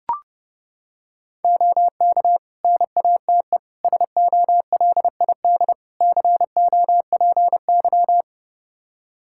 Morse code, heard as a keyed tone, about 30 wpm, 700 Hz. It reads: OK NATE SOLID COPY
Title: Practice QSO - Exchange 3 - More Info - 30wpm